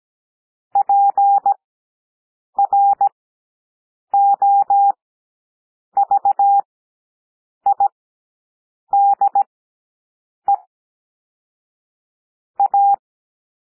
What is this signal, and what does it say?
Morse code, heard as a keyed tone, about 17 wpm, spaced as for 7 wpm, 800 Hz. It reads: PROVIDE A
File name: SOUND PLAIN 17 wpm